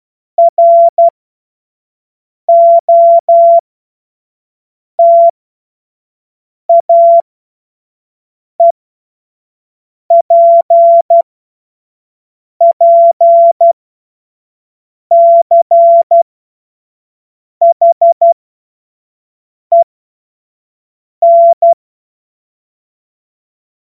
Morse code, heard as a keyed tone, about 12 wpm, spaced as for 5 wpm, 680 Hz. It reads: ROTAEPPCHEN
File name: Morsekurs / Witze / Witz 07. Juni 2022 25 F70